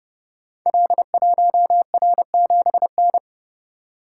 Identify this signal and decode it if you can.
Morse code, heard as a keyed tone, about 30 wpm, 700 Hz. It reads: L1R7N
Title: Code Groups - 5 Letter:Numbers - 30wpm